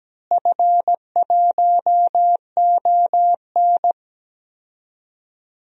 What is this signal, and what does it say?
Morse code, heard as a keyed tone, about 17 wpm, 700 Hz. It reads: F1ON
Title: DXCC-100 Call Signs - 4 Characters - Repeated 3x - 17wpm